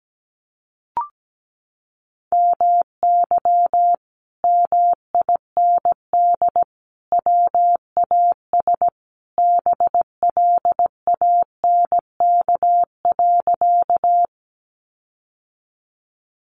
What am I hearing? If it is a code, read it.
Morse code, heard as a keyed tone, about 17 wpm, 700 Hz. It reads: MY MIND WAS BLANK.
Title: Sentences from Top 1500 Words 17wpm